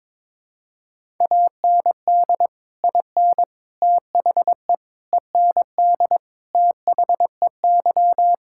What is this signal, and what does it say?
Morse code, heard as a keyed tone, about 21 wpm, 700 Hz. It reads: AND IN THE END THEY
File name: practice / 22wpm / Sentences with 2-Word Phrases - 22wpm